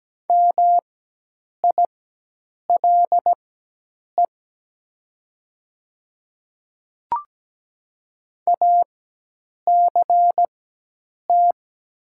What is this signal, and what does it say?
Morse code, heard as a keyed tone, about 17 wpm, spaced as for 8 wpm, 690 Hz. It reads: MILE ACT
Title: 4x Character Spacing - Top 300 Words - 17wpm